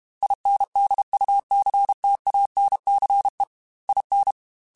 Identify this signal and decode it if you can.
Morse code, heard as a keyed tone, about 32 wpm, 800 Hz. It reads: INDUCTANCE IN
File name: Technical Subjects / Technical Subjects-mp3s / XL @30